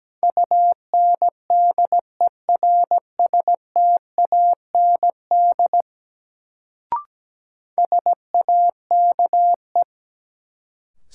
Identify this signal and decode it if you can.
Morse code, heard as a keyed tone, about 17 wpm, 700 Hz. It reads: UNDERSTAND SAKE
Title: Sets of 4 Words - New Word Drill - 17wpm